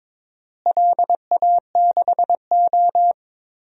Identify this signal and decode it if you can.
Morse code, heard as a keyed tone, about 22 wpm, 700 Hz. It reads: LA6O
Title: DXCC-100 Call Signs - 4 Characters - Repeated 2x - 22wpm